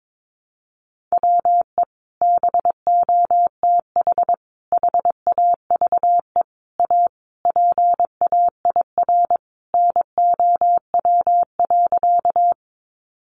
Morse code, heard as a keyed tone, about 22 wpm, 700 Hz. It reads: WE BOTH HAVE A PAIR NOW.
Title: Sentences from Top 1200 Words 22wpm